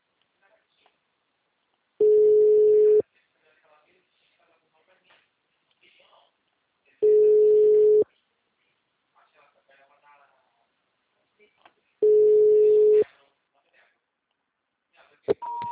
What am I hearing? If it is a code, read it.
Ringback tone (Europe, 425 Hz), with speech.
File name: out- -2044- - - .3238